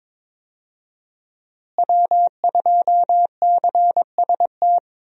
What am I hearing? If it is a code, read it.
Morse code, heard as a keyed tone, about 22 wpm, 700 Hz. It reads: W2CST
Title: USA Call Signs - 5 Characters - Repeated 2x - 22wpm